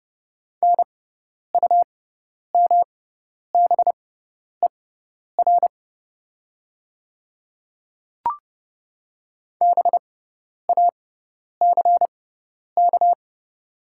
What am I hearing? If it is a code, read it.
Morse code, heard as a keyed tone, about 30 wpm, spaced as for 10 wpm, 700 Hz. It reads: NUMBER BACK
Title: 6x Character Spacing - Top 200 Words - 30wpm